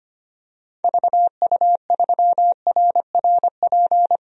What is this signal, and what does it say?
Morse code, heard as a keyed tone, about 25 wpm, 700 Hz. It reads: VU3RRP
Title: DXCC-100 Call Signs - 6 Characters - Repeated 2x - 25wpm